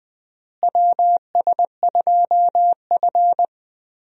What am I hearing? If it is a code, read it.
Morse code, heard as a keyed tone, about 21 wpm, 700 Hz. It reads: WS2F
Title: USA Call Signs - 4 Characters - Repeated 3x - 20wpm